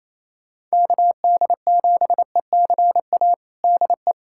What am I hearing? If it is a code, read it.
Morse code, heard as a keyed tone, about 28 wpm, 700 Hz. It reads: KD7ECA DE
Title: USA Callsign DE Callsign - 6 Characters - 28wpm